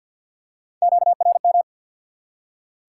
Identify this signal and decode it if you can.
Morse code, heard as a keyed tone, about 47 wpm, 700 Hz. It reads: QRM
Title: Top 90 QSO Elements 50wpm